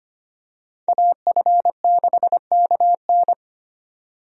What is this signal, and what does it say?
Morse code, heard as a keyed tone, about 25 wpm, 700 Hz. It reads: AF6KN
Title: USA Call Signs - 5 Characters - 25wpm